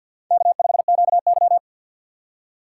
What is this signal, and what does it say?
Morse code, heard as a keyed tone, about 52 wpm, 700 Hz. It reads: K5QY
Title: USA Call Signs - 4 Characters - 2x Character Spacing - 50wpm